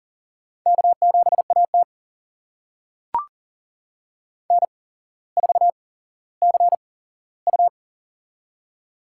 Morse code, heard as a keyed tone, about 40 wpm, spaced as for 17 wpm, 700 Hz. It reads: K7AT N4CU
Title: USA Call Signs - 4 Characters - 8x Character Spacing - 40wpm